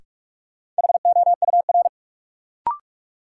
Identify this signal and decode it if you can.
Morse code, heard as a keyed tone, about 47 wpm, 700 Hz. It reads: HOUR
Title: Sentences from Top 400 Words - New Word Drill - 45wpm